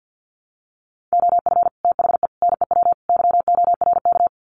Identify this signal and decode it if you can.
Morse code, heard as a keyed tone, about 50 wpm, 700 Hz. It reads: OF THE NEW YORK
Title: Sets of 4 Words 50wpm